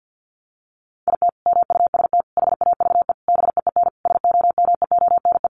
Morse code, heard as a keyed tone, about 50 wpm, 700 Hz. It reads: IT MUST HAVE BEEN SOMEONE
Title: Sentences from Top 800 Words 50wpm